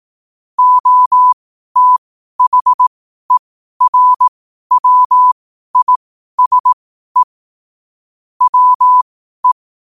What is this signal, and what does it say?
Morse code, heard as a keyed tone, about 18 wpm, spaced as for 12 wpm, 1000 Hz. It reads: OTHERWISE WE
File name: qotdfiles / 12wpm